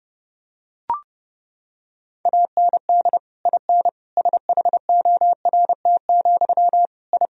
Morse code, heard as a keyed tone, about 30 wpm, 700 Hz. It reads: AND IN SHORT, I
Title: Sentences from Top 1000 Words - Review Part 2 - 30wpm